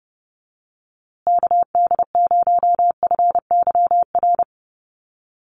Morse code, heard as a keyed tone, about 30 wpm, 700 Hz. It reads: KD0FYR
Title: USA Call Signs Repeated 2x 30wpm